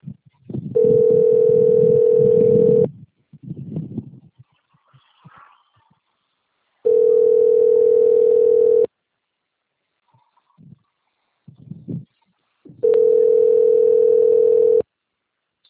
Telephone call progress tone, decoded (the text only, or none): ringback tone